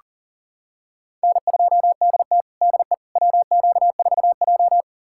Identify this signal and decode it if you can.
Morse code, heard as a keyed tone, about 40 wpm, 700 Hz. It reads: N2DT DE WQ4J
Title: USA Callsign DE Callsign - 4 Characters - 40wpm